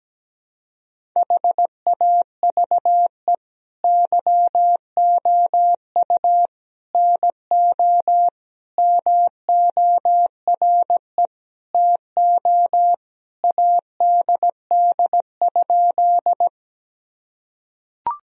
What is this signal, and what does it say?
Morse code, heard as a keyed tone, about 17 wpm, 700 Hz. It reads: HAVE YOU NO MORE TO ADD?
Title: Sentences from Top 1000 Words - Review Part 1 - 17wpm